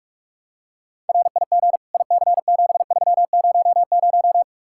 Morse code, heard as a keyed tone, about 46 wpm, 700 Hz. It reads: RIG IC7300